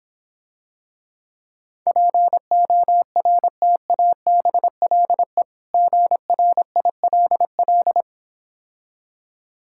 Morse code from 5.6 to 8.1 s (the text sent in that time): GRILL